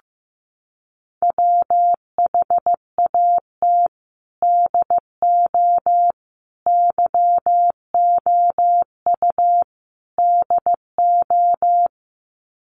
Morse code, heard as a keyed tone, about 15 wpm, 700 Hz. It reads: WHAT DO YOU DO
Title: Sets of 4 Words 15wpm